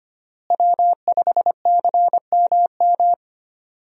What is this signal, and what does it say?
Morse code, heard as a keyed tone, about 25 wpm, 700 Hz. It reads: W5CMM
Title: USA Call Signs - 5 Characters - Repeated 2x - 25wpm